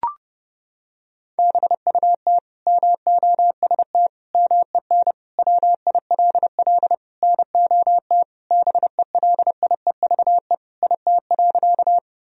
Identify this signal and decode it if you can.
Morse code, heard as a keyed tone, about 30 wpm, 700 Hz. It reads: BUT MOST MEN WILL NOT BELIEVE IT.
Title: Sentences from Top 1000 Words - Review Part 1 - 30wpm